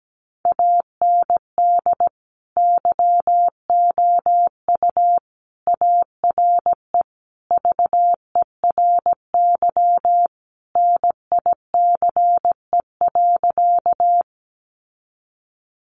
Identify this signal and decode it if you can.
Morse code, heard as a keyed tone, about 17 wpm, 700 Hz. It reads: AND YOU ARE VERY NICE.
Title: Sentences from Top 1100 Words 17wpm